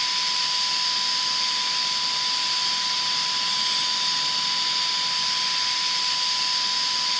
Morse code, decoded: none